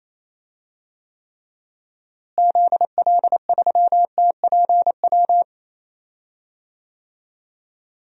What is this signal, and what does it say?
Morse code, heard as a keyed tone, about 27 wpm, 700 Hz. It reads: ZL3TPW
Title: DXCC-100 Call Signs - 6 Characters - Repeated 3x - 28wpm